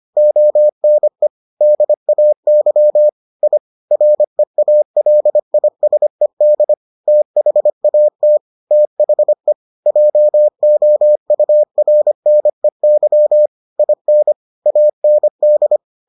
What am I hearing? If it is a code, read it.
Morse code, heard as a keyed tone, about 25 wpm, 600 Hz. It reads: ONE DAY I REALISED THAT THE JOURNEY IN AND